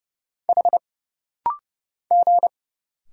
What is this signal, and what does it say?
Morse code, heard as a keyed tone, about 31 wpm, 700 Hz. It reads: H G
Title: Single Letters - Rapid-Fire 30wpm